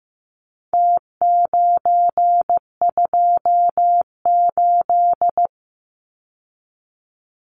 Morse code, heard as a keyed tone, about 15 wpm, 700 Hz. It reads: T928
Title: Numbers - Contest Style - 4-Digit 15wpm